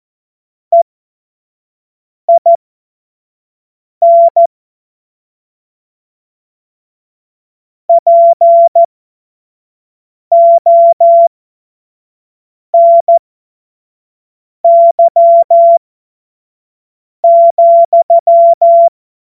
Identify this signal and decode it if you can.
Morse code, heard as a keyed tone, about 14 wpm, spaced as for 5 wpm, 680 Hz. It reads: EIN PONY,
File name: Morsekurs / Witze / Witz 06. Dezember 2022 25 F70